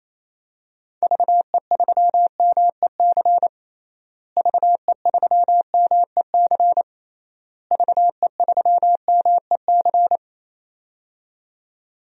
Morse code, heard as a keyed tone, about 28 wpm, 700 Hz. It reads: VE3MEC VE3MEC VE3MEC
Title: DXCC-100 Call Signs - 6 Characters - Repeated 3x - 28wpm